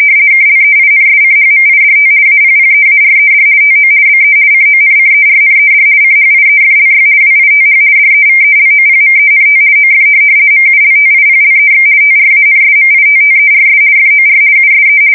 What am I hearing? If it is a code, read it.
RTTY, about 50 baud. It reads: KLEINSCHMIDT APPEARED TO BE CONCERNED CHIEFLY WITH DEVELOPMENT OF MULTIPLEX TRANSMITTERS FOR USE
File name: fsk170-50B/teletype-corp